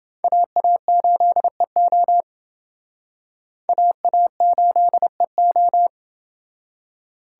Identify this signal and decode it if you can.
Morse code, heard as a keyed tone, about 28 wpm, 700 Hz. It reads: AA8EO AA8EO